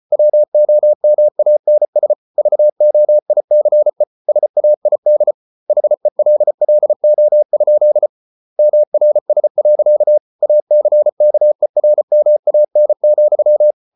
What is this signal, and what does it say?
Morse code, heard as a keyed tone, about 34 wpm, 600 Hz. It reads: WOMANS VOICE SAID HELLO? MRS. ACKERMAN,